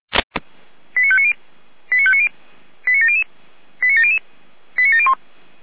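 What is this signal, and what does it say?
Five-tone selective call (ZVEI), heard as quick runs of five tones, tones of about 70 ms. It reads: 89590 89590 89700 89700 89701
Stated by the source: ZVEI five-tone call 89120, received on SDR.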